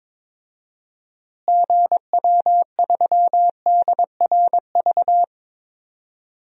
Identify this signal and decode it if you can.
Morse code, heard as a keyed tone, about 21 wpm, 700 Hz. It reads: GW3DRV
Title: DXCC-100 Call Signs - 6 Characters - Repeated 3x - 22wpm